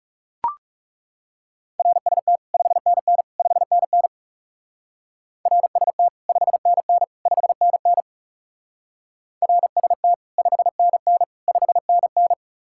Morse code, heard as a keyed tone, about 38 wpm, 700 Hz. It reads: RST 5NN 5NN RST 5NN 5NN RST 5NN 5NN